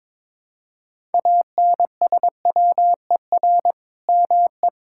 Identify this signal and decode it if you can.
Morse code, heard as a keyed tone, about 22 wpm, 700 Hz. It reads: ANSWER ME
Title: Sentences from Top 1000 Words - Review Part 5 - 22wpm